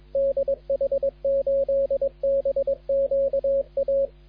Morse code, heard as a keyed tone, about 23 wpm, 250 Hz, with radio noise. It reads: DH8BQA
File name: HB9CZF de DH8BQA CQWWCW CW 21MHz